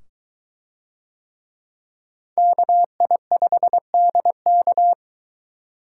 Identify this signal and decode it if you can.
Morse code, heard as a keyed tone, about 22 wpm, 700 Hz. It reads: KI5DK